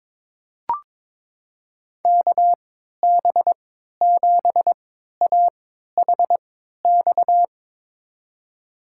Morse code, heard as a keyed tone, about 22 wpm, spaced as for 13 wpm, 700 Hz. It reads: KB7AHX
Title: USA Call Signs - 6 Characters - 3x Character Spacing - 22wpm